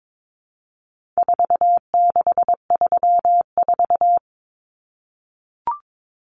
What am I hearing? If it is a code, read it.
Morse code, heard as a keyed tone, about 22 wpm, 700 Hz. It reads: 4634
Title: Numbers - 4-Digit 22wpm